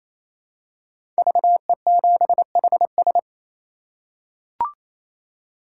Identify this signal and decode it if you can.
Morse code, heard as a keyed tone, about 28 wpm, 700 Hz. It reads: VE7HS
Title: K1USN SST QSO Practice - 28wpm